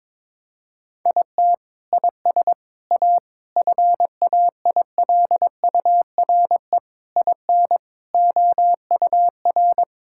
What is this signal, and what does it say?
Morse code, heard as a keyed tone, about 21 wpm, 700 Hz. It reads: IT IS A FAILURE IN OUR
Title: Sentences from Top 1000 Words - Review Part 6 - 22wpm